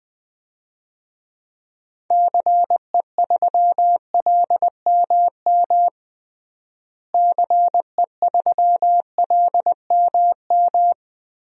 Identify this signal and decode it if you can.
Morse code, heard as a keyed tone, about 21 wpm, 700 Hz. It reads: CE3LMM CE3LMM
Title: DXCC-100 Call Signs - 6 Characters - Repeated 3x - 20wpm